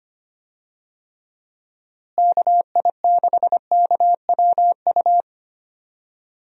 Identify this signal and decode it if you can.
Morse code, heard as a keyed tone, about 25 wpm, 700 Hz. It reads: KI6KWU
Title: USA Call Signs - 6 Characters - Repeated 2x - 25wpm